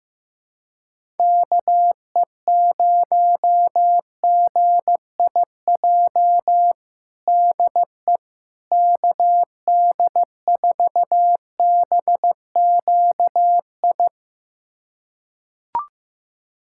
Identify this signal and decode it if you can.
Morse code, heard as a keyed tone, about 15 wpm, 700 Hz. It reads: KE0GIJ DE KD4BQI
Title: USA Callsign DE Callsign - 6 Characters - 15wpm